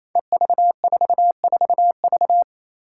Morse code, heard as a keyed tone, about 28 wpm, 700 Hz. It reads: EV44V
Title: Five Challenging Letter-Number Combinations 28wpm